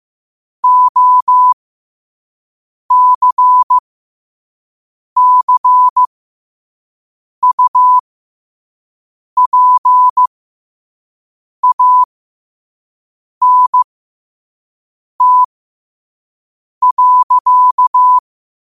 Morse code, heard as a keyed tone, about 15 wpm, spaced as for 5 wpm, 1000 Hz. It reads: OCCUPANT.
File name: qotdfiles / 5wpm